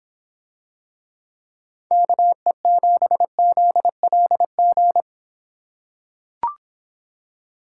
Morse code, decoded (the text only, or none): KE7ZLG